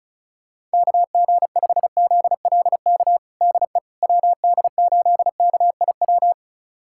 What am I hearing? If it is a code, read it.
Morse code, heard as a keyed tone, about 34 wpm, 700 Hz. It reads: KG5ZLK DE WD8KIW
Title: USA Callsign DE Callsign - 6 Characters - 35wpm